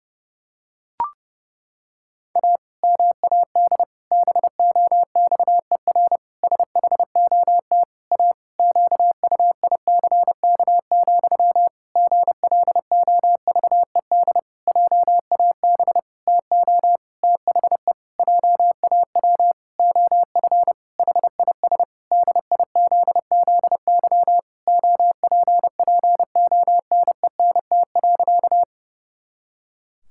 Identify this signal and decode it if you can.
Morse code, heard as a keyed tone, about 30 wpm, 700 Hz. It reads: A MAD BOXER SHOT A QUICK, GLOVED JAB TO THE JAW OF HIS DIZZY OPPONENT.
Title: Pangram Challenge v2 - 30wpm